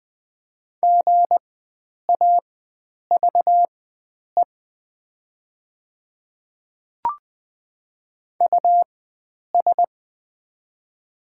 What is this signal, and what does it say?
Morse code, heard as a keyed tone, about 20 wpm, spaced as for 9 wpm, 700 Hz. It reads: GAVE US
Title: 4x Character Spacing - Top 500 Words - 20wpm